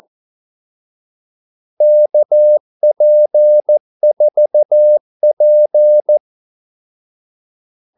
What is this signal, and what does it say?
Morse code, heard as a keyed tone, about 14 wpm, 600 Hz. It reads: KP4P
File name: Callsigns-14wpm